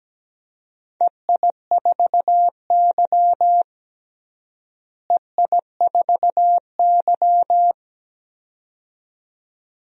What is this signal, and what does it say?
Morse code, heard as a keyed tone, about 17 wpm, 700 Hz. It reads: EI4Y EI4Y
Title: DXCC-100 Call Signs - 4 Characters - Repeated 2x - 17wpm